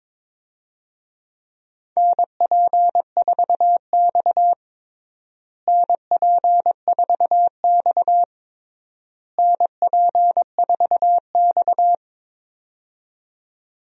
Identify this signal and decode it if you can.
Morse code, heard as a keyed tone, about 22 wpm, 700 Hz. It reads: NP4X NP4X NP4X
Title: USA Call Signs - 4 Characters - Repeated 3x - 22wpm